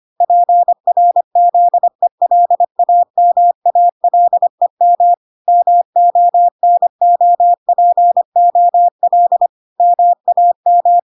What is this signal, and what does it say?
Morse code, heard as a keyed tone, about 25 wpm, 700 Hz. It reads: PRZELAMALEM MONOPOL MAM